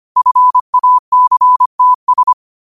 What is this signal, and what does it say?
Morse code, heard as a keyed tone, about 25 wpm, 1000 Hz. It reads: FACTS
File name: qotdfiles / 25wpm